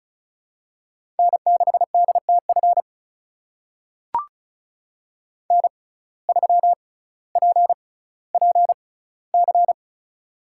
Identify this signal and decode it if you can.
Morse code, heard as a keyed tone, about 36 wpm, spaced as for 18 wpm, 700 Hz. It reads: N6DTF N3PPC
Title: USA Call Signs - 5 Characters - 6x Character Spacing - 35wpm